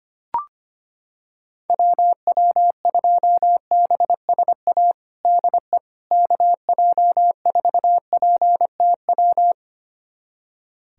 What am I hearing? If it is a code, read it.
Morse code, heard as a keyed tone, about 25 wpm, 700 Hz. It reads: WW2BSA DE KJ4PTW
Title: USA Callsign DE Callsign - 6 Characters - 25wpm